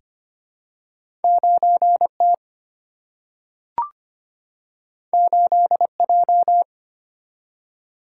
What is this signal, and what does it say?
Morse code, heard as a keyed tone, about 25 wpm, 700 Hz. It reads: 9 T 8 J
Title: Single-Letter Single-Number 25wpm